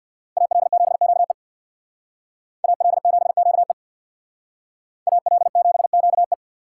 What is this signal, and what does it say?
Morse code, heard as a keyed tone, about 60 wpm, 700 Hz. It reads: AL7QE AL7QE AL7QE
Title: Speed Racing - USA 2x2 Call Signs 50wpm